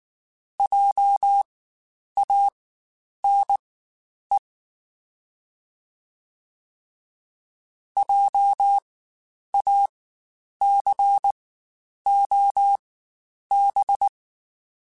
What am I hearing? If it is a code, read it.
Morse code, heard as a keyed tone, about 19 wpm, spaced as for 9 wpm, 800 Hz. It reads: JANE JACOB
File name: Spaced Alphas / Spaced Alphas mp3s / Js sp @18